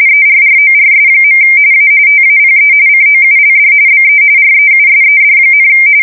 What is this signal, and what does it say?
RTTY, about 45.45 baud. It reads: ERRORS. A TAPE COULD BE REUSED,
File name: fsk170 / teletype-RTTY